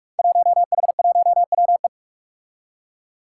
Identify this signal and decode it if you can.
Morse code, heard as a keyed tone, about 44 wpm, 700 Hz. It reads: 1H1WE
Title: Five Challenging Letter-Number Combinations 45wpm